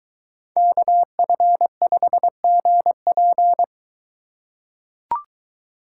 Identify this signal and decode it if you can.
Morse code, heard as a keyed tone, about 23 wpm, 700 Hz. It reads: KF5GP